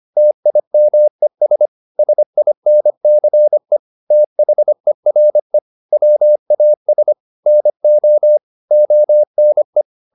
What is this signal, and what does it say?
Morse code, heard as a keyed tone, about 25 wpm, 600 Hz. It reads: TIMES SINCE THERE WAS NO ONE
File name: the old man and the sea part 1 25wpm